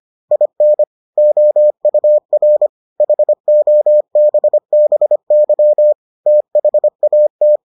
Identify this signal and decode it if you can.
Morse code, heard as a keyed tone, about 25 wpm, 600 Hz. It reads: IN OUR HOBBY THAT